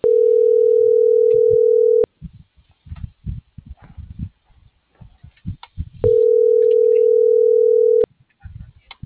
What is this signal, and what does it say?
Ringback tone (North America), 440 + 480 Hz.